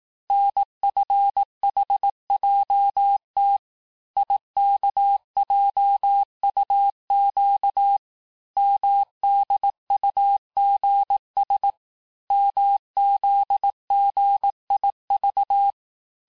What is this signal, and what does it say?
Morse code, heard as a keyed tone, about 18 wpm, 800 Hz. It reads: NFHJT IKJUQ MDUGS MZGIV
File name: cours-du-lundi files / cours-1594--teste-chiffre-vitesse-17mpm